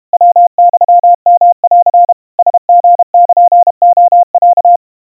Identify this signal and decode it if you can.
Morse code (Japanese Wabun code), heard as a keyed tone, about 32 wpm, 700 Hz. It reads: ヤユヨン ラリルレロ
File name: 110 cpm 700 Hz Crispy